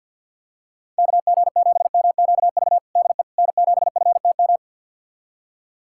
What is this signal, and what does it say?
Morse code, heard as a keyed tone, about 50 wpm, 700 Hz. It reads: KG7MQV DE N7FTM
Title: USA Callsign DE Callsign 50wpm